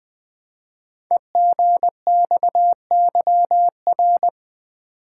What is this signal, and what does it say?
Morse code, heard as a keyed tone, about 20 wpm, 700 Hz. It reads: EGXYR
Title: Code Groups - 5 Letters - 20wpm